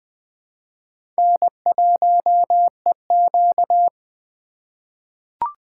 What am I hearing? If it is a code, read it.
Morse code, heard as a keyed tone, about 20 wpm, 700 Hz. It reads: N1EQ
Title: USA Call Signs - 4 Characters - 4x Character Spacing - 20wpm